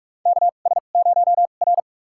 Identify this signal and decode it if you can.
Morse code, heard as a keyed tone, about 42 wpm, spaced as for 31 wpm, 700 Hz. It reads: KS0R
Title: USA Call Signs - 2x Character Spacing - 45wpm